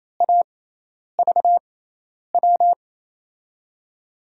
Morse code, heard as a keyed tone, about 28 wpm, spaced as for 10 wpm, 700 Hz. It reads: AVW
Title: USA Call Signs - 6x Character Spacing - 28wpm